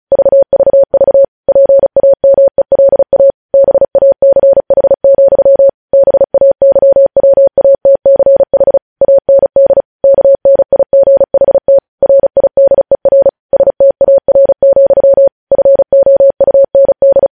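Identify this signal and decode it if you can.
Morse code, heard as a keyed tone, about 36 wpm, 550 Hz. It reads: VVV PAMELA BACH, BAYWATCH AND KNIGHT RIDER STAR, FOUND
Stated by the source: News headlines sent in Morse code at 35 WPM.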